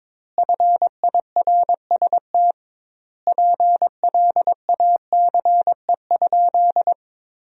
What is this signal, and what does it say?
Morse code, heard as a keyed tone, about 22 wpm, 700 Hz. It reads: FIRST PLACE?
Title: 2x Word-Spacing - Sentences from Top 200 Words - 22wpm